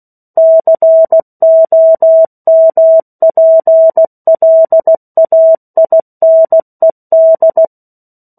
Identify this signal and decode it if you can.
Morse code, heard as a keyed tone, about 16 wpm, 650 Hz. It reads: COMPLAINED
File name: 03-Chapter-03 16 WPM